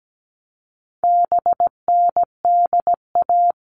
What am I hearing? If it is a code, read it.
Morse code, heard as a keyed tone, about 17 wpm, 700 Hz. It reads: BNDA
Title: All Cut Numbers - 4-Digit 17wpm